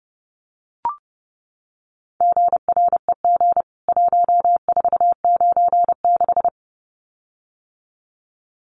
Morse code, heard as a keyed tone, about 30 wpm, 700 Hz. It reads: GREG 1496
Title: CWT Exchange 30wpm